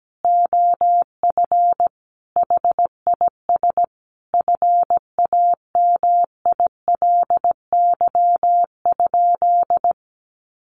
Morse code, heard as a keyed tone, about 17 wpm, 700 Hz. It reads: OF HIS FAMILY?
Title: Sentences from Top 1300 Words 17wpm